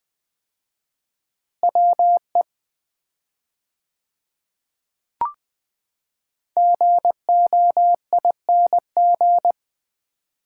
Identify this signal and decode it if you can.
Morse code, heard as a keyed tone, about 20 wpm, 700 Hz. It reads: WE GOING
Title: 4x Word-Spacing - Sets of 3 Words - 20wpm